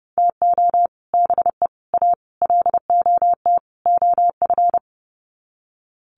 Morse code, heard as a keyed tone, about 30 wpm, 700 Hz. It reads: TO BE A LOT OF
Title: Sets of 5 Words Encore 30wpm